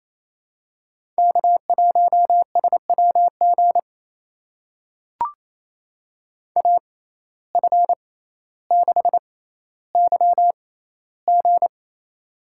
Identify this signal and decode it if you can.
Morse code, heard as a keyed tone, about 28 wpm, spaced as for 14 wpm, 700 Hz. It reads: K1SWG AF6YG
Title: USA Call Signs - 5 Characters - 6x Character Spacing - 28wpm